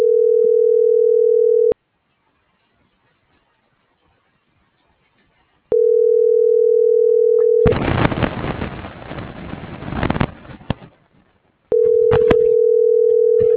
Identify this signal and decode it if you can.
Ringback tone (North America), 440 + 480 Hz.